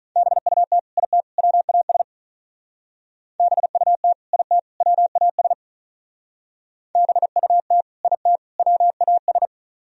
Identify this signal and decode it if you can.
Morse code, heard as a keyed tone, about 41 wpm, 700 Hz. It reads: BUT IT WAS BUT IT WAS BUT IT WAS
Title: Speed-Racing - Sets of 3 Words 35wpm